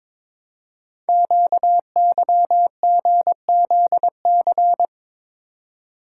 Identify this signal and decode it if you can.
Morse code, heard as a keyed tone, about 23 wpm, 700 Hz. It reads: QYGZC
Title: Code Groups - 5 Letters - 22wpm